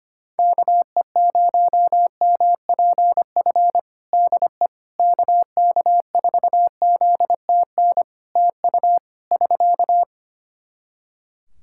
Morse code, heard as a keyed tone, about 25 wpm, 700 Hz. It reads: KE0MPF DE KK4ZTN TU <SK>
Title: Practice QSO - Exchange 4 - Ending - 25wpm